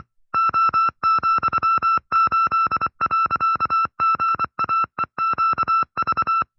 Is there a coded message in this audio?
Yes: Morse code from 0.3 to 6.4 s, heard as a keyed tone, about 25 wpm, 1330 Hz. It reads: O,8.GAEQV